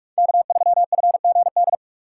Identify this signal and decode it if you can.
Morse code, heard as a keyed tone, about 46 wpm, 700 Hz. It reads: K3FGD
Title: USA Call Signs - 2x Character Spacing - 45wpm